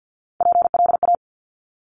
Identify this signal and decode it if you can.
Morse code, heard as a keyed tone, about 50 wpm, 700 Hz. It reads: PLA
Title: Top Three Letter Combinations in Words 50wpm